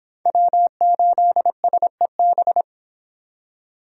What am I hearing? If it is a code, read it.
Morse code, heard as a keyed tone, about 26 wpm, 700 Hz. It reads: W8SEB
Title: Speed Racing - USA 1x3 Call Signs 22wpm